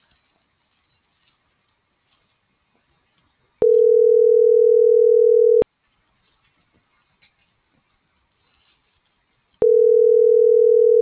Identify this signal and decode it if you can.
Ringback tone (North America), 440 + 480 Hz.